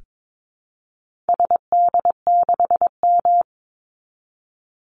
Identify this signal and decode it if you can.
Morse code, heard as a keyed tone, about 22 wpm, 700 Hz. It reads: SD6M
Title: Call Signs - 22wpm